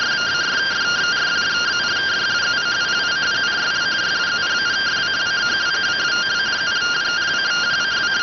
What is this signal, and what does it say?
Two-tone teleprinter signal, about 170 Hz shift.